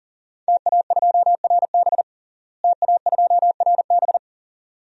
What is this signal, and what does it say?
Morse code, heard as a keyed tone, about 40 wpm, 700 Hz. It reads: TA2RB TA2RB
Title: DXCC-100 Call Signs - 5 Characters - Repeated 2x - 40wpm